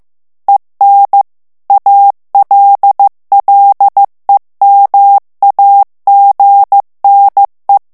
Morse code, heard as a keyed tone, about 15 wpm, 800 Hz. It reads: EN ALLEMAGNE
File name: morse mp3 / FRANCAIS TEXTE CLAIR 1 VITESSE 14 MOTS MINUTES